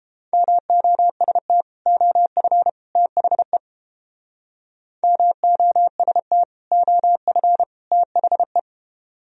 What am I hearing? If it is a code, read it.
Morse code, heard as a keyed tone, about 30 wpm, 700 Hz. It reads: MOST OF THE MOST OF THE